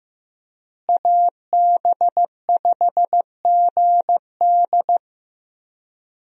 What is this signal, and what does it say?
Morse code, heard as a keyed tone, about 15 wpm, 700 Hz. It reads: AB5GD
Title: USA Call Signs - 5 Characters - Repeated 2x - 15wpm